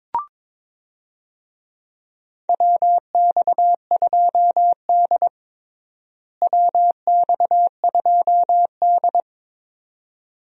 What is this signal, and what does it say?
Morse code, heard as a keyed tone, about 22 wpm, 700 Hz. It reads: WX2D WX2D
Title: USA Call Signs - 4 Characters - Repeated 2x - 22wpm